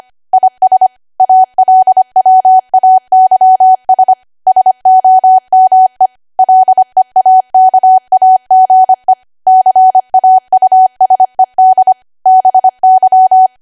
Morse code, heard as a keyed tone, about 25 wpm, 750 Hz. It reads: IS ALWAYS SOME LEAKAGE CAUSED BY
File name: cwa / W1AW-Level3 / 25WPM